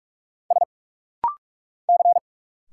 Morse code, heard as a keyed tone, about 46 wpm, 700 Hz. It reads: S C
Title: Single Letters - Rapid-Fire 45wpm